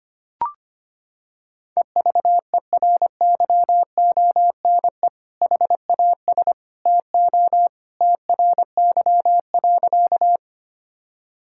Morse code, heard as a keyed tone, about 25 wpm, 700 Hz. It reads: EVERYONE HAS TO TRY.
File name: practice / 25wpm / Sentences with 2-Word Phrases - 25wpm